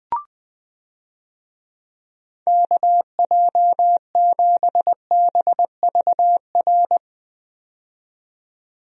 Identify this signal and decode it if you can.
Morse code, heard as a keyed tone, about 20 wpm, 700 Hz. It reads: KJ7BVR
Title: USA Call Signs - 6 Characters - 20wpm